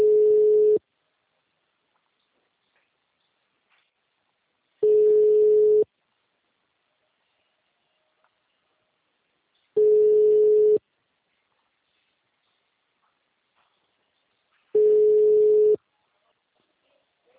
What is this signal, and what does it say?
Ringback tone (Europe, 425 Hz).